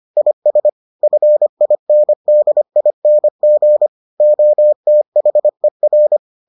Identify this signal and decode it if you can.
Morse code, heard as a keyed tone, about 25 wpm, 600 Hz. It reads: IS FINDING OTHER